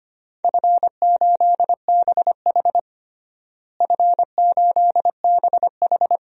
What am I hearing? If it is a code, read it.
Morse code, heard as a keyed tone, about 25 wpm, 700 Hz. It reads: F8BH F8BH
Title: DXCC-100 Call Signs - 4 Characters - Repeated 3x - 25wpm